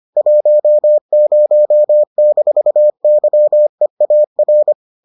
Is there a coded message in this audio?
Yes: Morse code from 0.2 to 4.7 s, heard as a keyed tone, about 25 wpm, 600 Hz. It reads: 10-YEAR